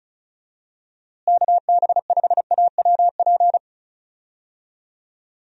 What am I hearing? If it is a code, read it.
Morse code, heard as a keyed tone, about 36 wpm, 700 Hz. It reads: KB5AWP
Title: USA Call Signs - 6 Characters - Repeated 2x - 35wpm